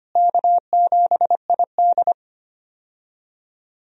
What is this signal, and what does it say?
Morse code, heard as a keyed tone, about 25 wpm, 700 Hz. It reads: K7ID
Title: USA Call Signs - 4 Characters - Repeated 2x - 25wpm